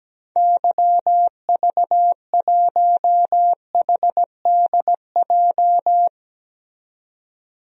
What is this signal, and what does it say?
Morse code, heard as a keyed tone, about 17 wpm, 700 Hz. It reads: YV1HDJ
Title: DXCC-100 Call Signs - 6 Characters - 17wpm